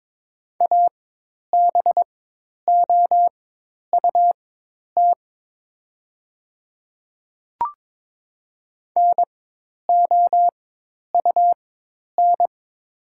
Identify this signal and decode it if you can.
Morse code, heard as a keyed tone, about 21 wpm, spaced as for 10 wpm, 690 Hz. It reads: ABOUT NOUN
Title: 4x Character Spacing - Top 500 Words - 22wpm